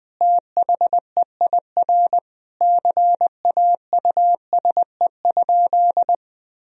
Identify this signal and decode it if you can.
Morse code, heard as a keyed tone, about 20 wpm, 700 Hz. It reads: THEIR CAUSE?
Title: Sentences from Top 1000 Words - Review Part 1 - 20wpm